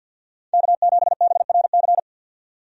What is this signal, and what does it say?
Morse code, heard as a keyed tone, about 50 wpm, 700 Hz. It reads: K7BRC
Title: USA Call Signs - 5 Characters - 50wpm